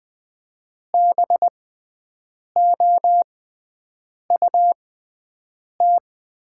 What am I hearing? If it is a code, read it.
Morse code, heard as a keyed tone, about 20 wpm, spaced as for 7 wpm, 690 Hz. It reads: BOUT
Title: 6x Character Spacing - Top 100 Words - 20wpm